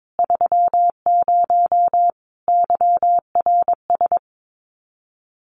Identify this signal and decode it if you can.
Morse code, heard as a keyed tone, about 22 wpm, 700 Hz. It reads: 30 YRS
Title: Common QSO Phrases Encore 22wpm